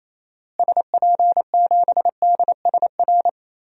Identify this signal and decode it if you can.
Morse code, heard as a keyed tone, about 28 wpm, 700 Hz. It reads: SP7DSR
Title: DXCC-100 Call Signs - 6 Characters - Repeated 3x - 28wpm